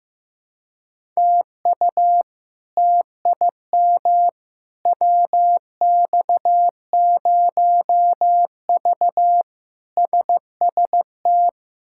Morse code, heard as a keyed tone, about 15 wpm, 700 Hz. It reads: TU TIM WX0V SST
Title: K1USN SST QSO Practice - 15wpm